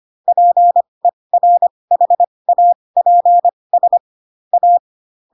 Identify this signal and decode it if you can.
Morse code, heard as a keyed tone, about 25 wpm, spaced as for 20 wpm, 700 Hz. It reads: PERHAPS A